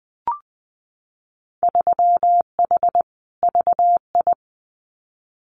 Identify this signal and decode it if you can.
Morse code, heard as a keyed tone, about 20 wpm, 700 Hz. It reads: 3H VI
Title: Winter Field Day Exchanges Encore 20wpm